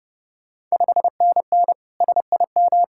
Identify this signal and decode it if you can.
Morse code, heard as a keyed tone, about 32 wpm, 700 Hz. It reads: 5NN SIM
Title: Ontario QSO Party Exchange 30wpm